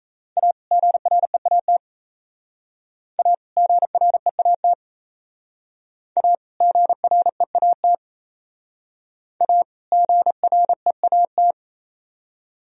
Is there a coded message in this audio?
Yes: Morse code from 0.4 to 11.5 s, heard as a keyed tone, about 35 wpm, 700 Hz. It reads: A GREAT A GREAT A GREAT A GREAT